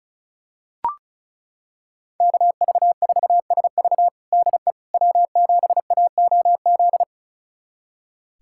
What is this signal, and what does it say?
Morse code, heard as a keyed tone, about 36 wpm, 700 Hz. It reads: KV4SV DE W7AOZ
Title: USA Callsign DE Callsign - 5 Characters - 35wpm